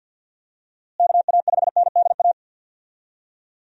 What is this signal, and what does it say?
Morse code, heard as a keyed tone, about 50 wpm, 700 Hz. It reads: KA5NDA
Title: USA Call Signs - 6 Characters - Repeated 3x - 50wpm